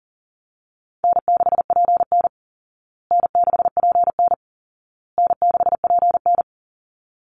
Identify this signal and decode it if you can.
Morse code, heard as a keyed tone, about 40 wpm, 700 Hz. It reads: N6PN N6PN N6PN
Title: Easy Call Signs Repeated 3x - 40wpm